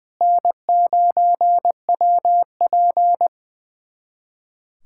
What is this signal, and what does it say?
Morse code, heard as a keyed tone, about 21 wpm, 700 Hz. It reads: N9WP
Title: USA Callsign DE Callsign - 4 Characters - 20wpm